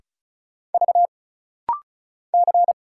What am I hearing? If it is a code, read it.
Morse code, heard as a keyed tone, about 36 wpm, 700 Hz. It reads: V C
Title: Single Letter-Number Alt - Rapid-Fire 35wpm